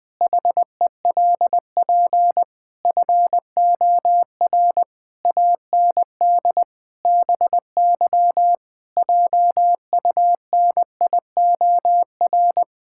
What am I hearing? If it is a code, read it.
Morse code, heard as a keyed tone, about 20 wpm, 700 Hz. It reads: HELP FOR AND BY JUNIOR
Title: Pangram Challenge v2 - 20wpm